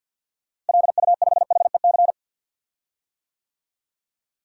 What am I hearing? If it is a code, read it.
Morse code, heard as a keyed tone, about 52 wpm, 700 Hz. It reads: LU5HEC
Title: DXCC-100 Call Signs - 6 Characters - Repeated 2x - 50wpm